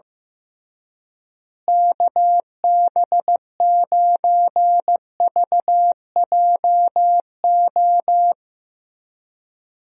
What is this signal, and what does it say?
Morse code, heard as a keyed tone, about 15 wpm, 700 Hz. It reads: KB9VJO
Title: USA Call Signs - 6 Characters - Repeated 3x - 15wpm